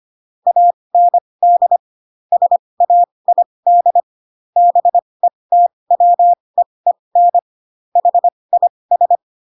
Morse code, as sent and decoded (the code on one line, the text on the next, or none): .- -. -.. / ... .- .. -.. / -... - .-- . -. / .... .. ...
AND SAID BETWEEN HIS